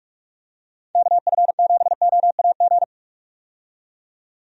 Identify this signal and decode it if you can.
Morse code, heard as a keyed tone, about 45 wpm, 700 Hz. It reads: KF7OAG
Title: USA Call Signs - 6 Characters - 45wpm